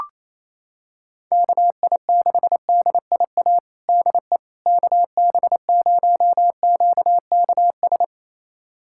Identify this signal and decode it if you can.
Morse code, heard as a keyed tone, about 28 wpm, 700 Hz. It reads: KI6DIA DE KB0QKS
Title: USA Callsign DE Callsign 28wpm